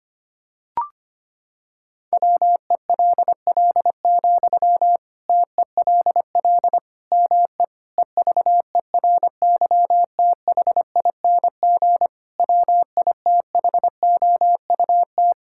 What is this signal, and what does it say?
Morse code, heard as a keyed tone, about 25 wpm, 700 Hz. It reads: WELL, TELL ME EVERYTHING WITHOUT
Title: Sentences from Top 1000 Words - Review Part 6 - 25wpm